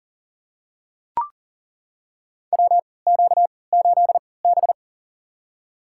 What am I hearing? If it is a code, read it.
Morse code, heard as a keyed tone, about 38 wpm, spaced as for 22 wpm, 700 Hz. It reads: WQ8B
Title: USA Call Signs - 4 Characters - 3x Character Spacing - 40wpm